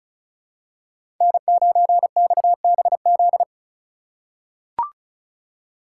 Morse code, heard as a keyed tone, about 34 wpm, 700 Hz. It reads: N9XBZ
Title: USA Call Signs - 5 Characters - 2x Character Spacing - 35wpm